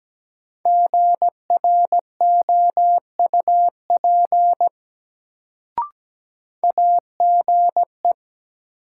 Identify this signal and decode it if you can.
Morse code, heard as a keyed tone, about 17 wpm, 700 Hz. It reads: GROUP AGE
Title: Sets of 3 Words - New Word Drill - 17wpm